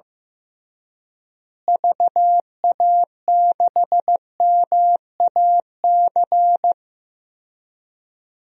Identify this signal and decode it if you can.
Morse code, heard as a keyed tone, about 15 wpm, 700 Hz. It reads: VA6MAC
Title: DXCC-100 Call Signs - 6 Characters - Repeated 3x - 15wpm